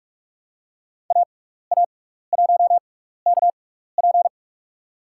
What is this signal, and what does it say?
Morse code, heard as a keyed tone, about 44 wpm, spaced as for 15 wpm, 700 Hz. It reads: AA1KP
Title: USA Call Signs - 5 Characters - 6x Character Spacing - 45wpm